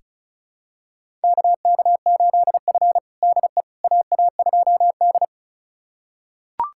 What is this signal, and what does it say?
Morse code, heard as a keyed tone, about 36 wpm, 700 Hz. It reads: KK8F DE AA2D
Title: USA Callsign DE Callsign - 4 Characters - 35wpm